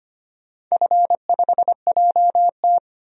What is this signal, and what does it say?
Morse code, heard as a keyed tone, about 25 wpm, 700 Hz. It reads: F5JT
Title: DXCC-100 Call Signs - 4 Characters - 25wpm